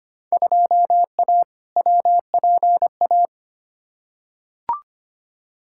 Morse code, heard as a keyed tone, about 25 wpm, 700 Hz. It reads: 2A WPA
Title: Field Day Exchanges - 25wpm